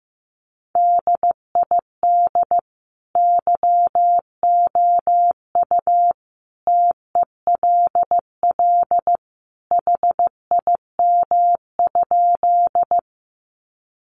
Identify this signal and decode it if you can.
Morse code, heard as a keyed tone, about 15 wpm, 700 Hz. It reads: DID YOU TELL HIM?
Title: Sentences from Top 200 Words 15wpm